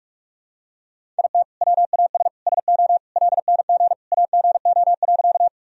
Morse code, heard as a keyed tone, about 41 wpm, 700 Hz. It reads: IT WAS SO LNG AGO.
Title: Sentences from Top 500 Words and Abbreviations 45wpm